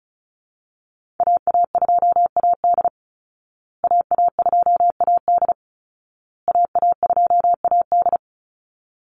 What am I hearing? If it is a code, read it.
Morse code, heard as a keyed tone, about 34 wpm, 700 Hz. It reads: AA2AD AA2AD AA2AD
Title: Call Signs Repeated 2x - 35wpm